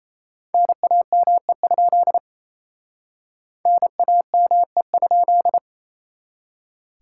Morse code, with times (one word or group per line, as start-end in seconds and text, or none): 0.5-2.2 NAME?
3.7-5.6 NAME?